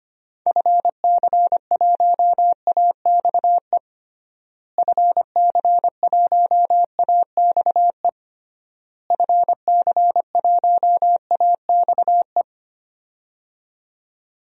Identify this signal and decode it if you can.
Morse code, heard as a keyed tone, about 25 wpm, 700 Hz. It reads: FC1AXE FC1AXE FC1AXE
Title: DXCC-100 Call Signs - 6 Characters - Repeated 3x - 25wpm